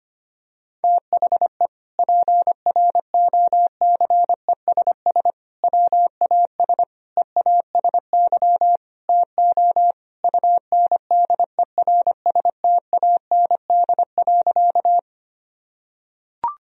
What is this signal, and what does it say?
Morse code, heard as a keyed tone, about 25 wpm, 700 Hz. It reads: THE PROCESS WAS EASY TO UNDERSTAND.
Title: Sentences from Top 1000 Words - Review Part 3 - 25wpm